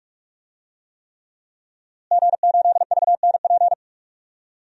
Morse code, heard as a keyed tone, about 46 wpm, 700 Hz. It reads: G8VNP
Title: DXCC-100 Call Signs - 5 Characters - Repeated 3x - 45wpm